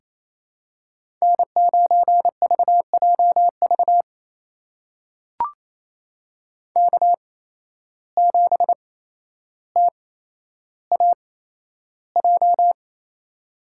Morse code, heard as a keyed tone, about 28 wpm, spaced as for 12 wpm, 700 Hz. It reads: N9VJV K7TAJ
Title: USA Call Signs - 5 Characters - 8x Character Spacing - 28wpm